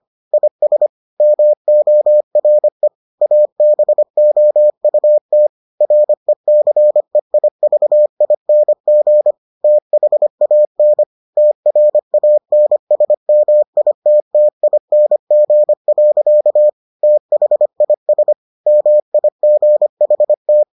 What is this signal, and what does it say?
Morse code, heard as a keyed tone, about 25 wpm, 600 Hz. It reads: IS MORE ABOUT RECEIVING THAN TRANSMITTING. THIS MIGHT